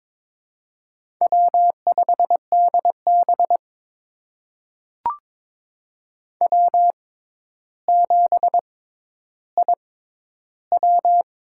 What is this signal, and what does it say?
Morse code, heard as a keyed tone, about 22 wpm, spaced as for 11 wpm, 700 Hz. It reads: W5DB W7IW
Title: USA Call Signs - 4 Characters - 6x Character Spacing - 22wpm